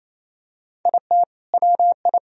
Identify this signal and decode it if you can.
Morse code, heard as a keyed tone, about 28 wpm, 700 Hz. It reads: IT WI
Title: Sentences from Top 1000 Words - Review Part 4 - 28wpm